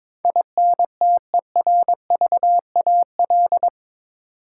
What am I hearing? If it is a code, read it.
Morse code, heard as a keyed tone, about 21 wpm, 700 Hz. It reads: INTERVAL
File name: practice / 22wpm / Eight Letter Words - 22wpm